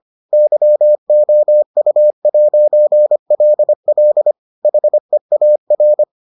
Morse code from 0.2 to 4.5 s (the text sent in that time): YOU'LL